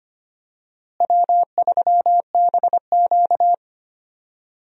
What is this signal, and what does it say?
Morse code, heard as a keyed tone, about 25 wpm, 700 Hz. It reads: W3BQ
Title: USA Call Signs - 4 Characters - Repeated 2x - 25wpm